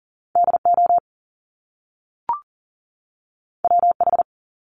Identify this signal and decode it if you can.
Morse code, heard as a keyed tone, about 39 wpm, 700 Hz. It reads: DO WH
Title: Top Two Letter Combinations in Words 40wpm